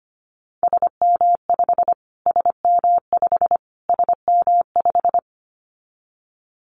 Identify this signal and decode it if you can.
Morse code, heard as a keyed tone, about 25 wpm, 700 Hz. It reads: SM5 SM5 SM5
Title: Callsign Prefixes for Top 25 DXCC - Repeated 3x - 25wpm